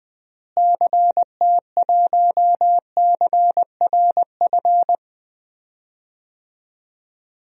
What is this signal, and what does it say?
Morse code, heard as a keyed tone, about 20 wpm, 700 Hz. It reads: CT1CRF
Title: DXCC-100 Call Signs - 6 Characters - Repeated 2x - 20wpm